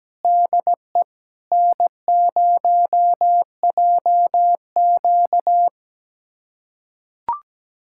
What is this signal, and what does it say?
Morse code, heard as a keyed tone, about 17 wpm, 700 Hz. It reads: DE N0JQ
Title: USA Callsign DE Callsign - 4 Characters - 17wpm